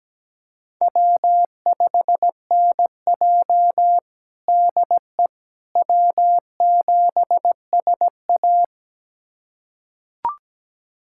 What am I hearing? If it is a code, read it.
Morse code, heard as a keyed tone, about 17 wpm, 700 Hz. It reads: W5NJ DE W7SA
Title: USA Callsign DE Callsign - 4 Characters - 17wpm